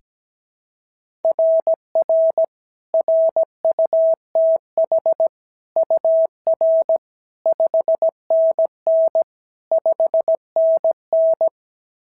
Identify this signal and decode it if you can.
Morse code, heard as a keyed tone, about 17 wpm, 660 Hz. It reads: RR RUTH UR 5NN 5NN
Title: SOTA QSO Practice 17wpm